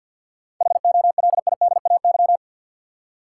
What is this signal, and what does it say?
Morse code, heard as a keyed tone, about 49 wpm, 700 Hz. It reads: HOLIDAY
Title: Seven Letter Words - 50wpm